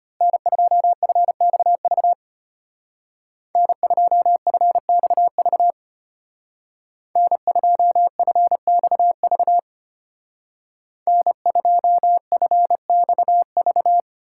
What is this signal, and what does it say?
Morse code, heard as a keyed tone, about 31 wpm, 700 Hz. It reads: N2FXV N2FXV N2FXV N2FXV
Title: Speed Racing - USA 1x3 Call Signs 25wpm